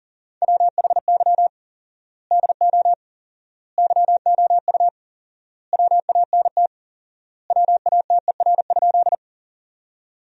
Morse code, heard as a keyed tone, about 40 wpm, 700 Hz. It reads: WHY DO YOU WANT WATER?
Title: 4x Word-Spacing - Sentences from Top 200 Words - 40wpm